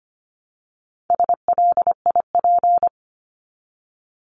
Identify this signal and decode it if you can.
Morse code, heard as a keyed tone, about 25 wpm, 700 Hz. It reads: SLIP
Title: Top 1751-2000 Words 25wpm